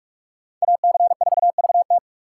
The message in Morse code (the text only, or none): AC4VT